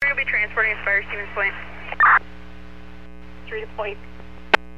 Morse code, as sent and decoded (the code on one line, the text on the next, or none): none